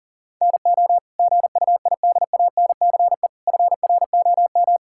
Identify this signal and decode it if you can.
Morse code, heard as a keyed tone, about 38 wpm, 700 Hz. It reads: NO GUIDANCE FROM
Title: Sentences from Top 1600 Words 40wpm